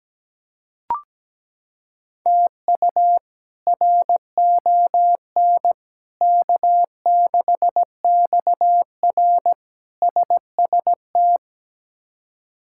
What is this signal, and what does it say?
Morse code, heard as a keyed tone, about 17 wpm, 700 Hz. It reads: TU RON K6XR SST
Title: K1USN SST QSO Practice - 17wpm